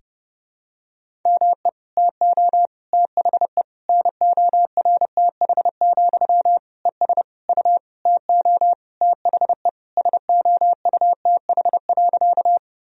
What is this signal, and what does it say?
Morse code, heard as a keyed tone, about 30 wpm, 700 Hz. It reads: ME TO THE NORTH, ES U TO THE SOUTH.
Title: Sentences from Top 400 Words and Abbreviations 30wpm